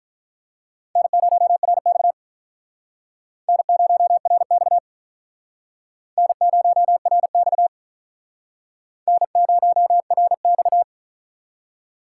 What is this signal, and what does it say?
Morse code, heard as a keyed tone, about 46 wpm, 700 Hz. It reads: N0RX N0RX N0RX N0RX